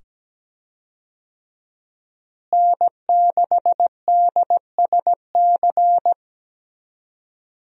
Morse code, heard as a keyed tone, about 17 wpm, 700 Hz. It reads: N6DSC
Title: USA Call Signs - 5 Characters - Repeated 2x - 17wpm